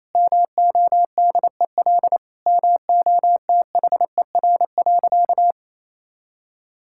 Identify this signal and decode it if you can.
Morse code, heard as a keyed tone, about 28 wpm, 700 Hz. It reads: MODEL MOTHER.
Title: Sentences from Top 1000 Words - Review Part 5 - 28wpm